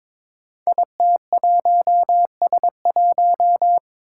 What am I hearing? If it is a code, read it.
Morse code, heard as a keyed tone, about 22 wpm, 700 Hz. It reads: IT1S1
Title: Five Challenging Letter-Number Combinations 22wpm